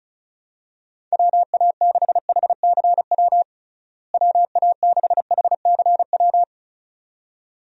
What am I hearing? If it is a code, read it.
Morse code, heard as a keyed tone, about 36 wpm, 700 Hz. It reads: WA6HCW WA6HCW
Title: USA Call Signs - 6 Characters - Repeated 2x - 35wpm